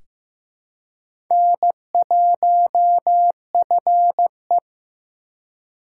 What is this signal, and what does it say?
Morse code, heard as a keyed tone, about 15 wpm, 700 Hz. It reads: N1FE
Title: K1USN SST QSO Practice - 15wpm